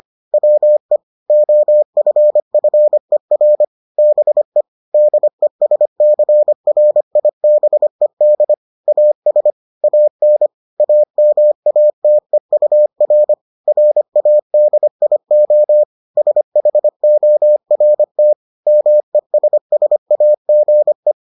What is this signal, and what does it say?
Morse code, heard as a keyed tone, about 25 wpm, 600 Hz. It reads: WE OFFER BE DESCRIBED AS AN AMATEUR RADIO SHORT MESSAGE